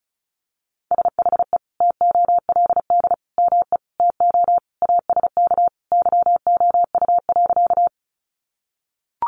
Morse code, heard as a keyed tone, about 34 wpm, 700 Hz. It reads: SHE TOLD ME TO ASK YOU.
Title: Sentences from Top 400 Words 35wpm